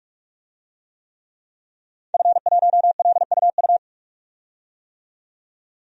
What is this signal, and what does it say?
Morse code, heard as a keyed tone, about 46 wpm, 700 Hz. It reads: F1LUU
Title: DXCC-100 Call Signs - 5 Characters - 45wpm